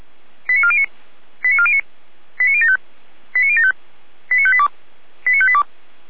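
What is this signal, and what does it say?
Five-tone selective call (ZVEI), heard as quick runs of five tones, tones of about 70 ms. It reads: 89409 89409 89075 89075 89582 89582
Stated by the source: SDR-received ZVEI five-tone call 89018.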